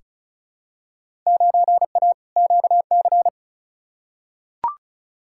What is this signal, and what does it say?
Morse code, heard as a keyed tone, about 36 wpm, 700 Hz. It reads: 9A QC
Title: Field Day Exchanges Encore - 35wpm